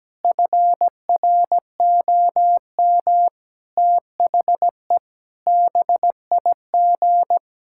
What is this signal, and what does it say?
Morse code, heard as a keyed tone, about 17 wpm, 700 Hz. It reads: FROM THE BIG
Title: Pangram Challenge v2 - 17wpm